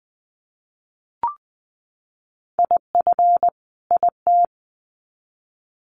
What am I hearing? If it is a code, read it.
Morse code, heard as a keyed tone, about 20 wpm, 700 Hz. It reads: IF IT
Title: Sets of 2 Words 20wpm